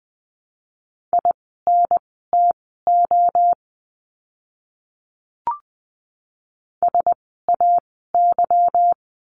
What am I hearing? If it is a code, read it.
Morse code, heard as a keyed tone, about 20 wpm, spaced as for 15 wpm, 700 Hz. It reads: INTO SAY
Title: 237 Lesson 38f - Letter q - Words 20wpm